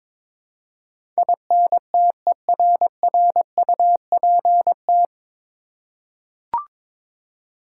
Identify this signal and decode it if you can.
Morse code, heard as a keyed tone, about 22 wpm, 700 Hz. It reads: INTERRUPT
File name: 9 Letter Words - First 3 Letters Given - 22wpm